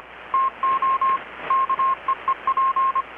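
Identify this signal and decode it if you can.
Morse code, heard as a keyed tone, about 25 wpm, 1070 Hz, with radio noise. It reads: TO KEEP
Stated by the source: Morse from KPH's Night of Nights XX maritime radiotelegraphy event, received in Chiapas, Mexico about 3,800 km away on a Kenwood R-600 with a grounded 10 m long-wire antenna.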